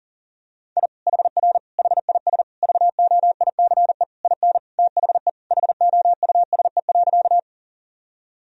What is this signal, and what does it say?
Morse code, heard as a keyed tone, about 37 wpm, 700 Hz. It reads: I HR HIS VOICE IN THE HOUSE.
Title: Sentences from Top 500 Words and Abbreviations 40wpm